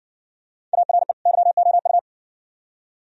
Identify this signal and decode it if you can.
Morse code, heard as a keyed tone, about 60 wpm, 700 Hz. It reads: ARE YOU